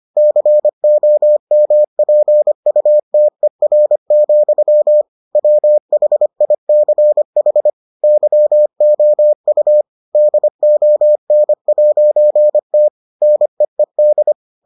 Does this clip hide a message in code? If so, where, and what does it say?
Yes: Morse code from 0.2 to 14.3 s, heard as a keyed tone, about 25 wpm, 600 Hz. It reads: COMPUTER, WHICH YOU DON'T NEED